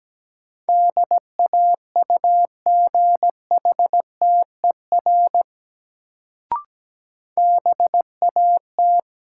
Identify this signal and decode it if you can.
Morse code, heard as a keyed tone, about 17 wpm, 700 Hz. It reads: DAUGHTER BAT
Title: Sentences from Top 1500 Words - New Word Drill Alt - 17wpm